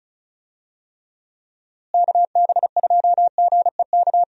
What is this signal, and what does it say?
Morse code, heard as a keyed tone, about 36 wpm, 700 Hz. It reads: KB2GEK
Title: USA Call Signs - 6 Characters - Repeated 2x - 35wpm